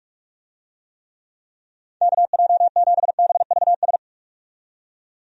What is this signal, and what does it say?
Morse code, heard as a keyed tone, about 46 wpm, 700 Hz. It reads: KJ7BVS
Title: USA Call Signs - 6 Characters - 45wpm